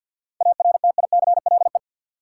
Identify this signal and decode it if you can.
Morse code, heard as a keyed tone, about 52 wpm, 700 Hz. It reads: ARTICLE
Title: Sentences from Top 1000 Words - Review Part 5 - New Word Drill - 50wpm